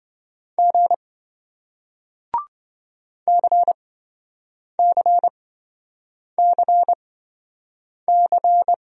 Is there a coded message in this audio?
Yes: Morse code from 0.6 to 8.7 s, heard as a keyed tone, about 28 wpm, spaced as for 7 wpm, 700 Hz. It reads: G CCCC